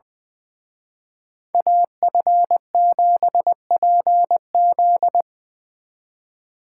Morse code, heard as a keyed tone, about 24 wpm, 700 Hz. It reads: AF7PZ
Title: Speed Racing - USA 2x2 Call Signs 20wpm